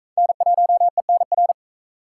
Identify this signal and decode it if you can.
Morse code, heard as a keyed tone, about 44 wpm, 700 Hz. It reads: N1ENR